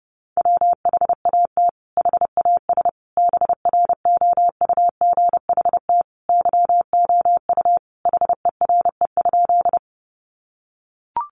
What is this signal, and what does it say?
Morse code, heard as a keyed tone, about 30 wpm, 700 Hz. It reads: WHAT HAS BROUGHT YOU HERE?
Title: Sentences from Top 500 Words 30wpm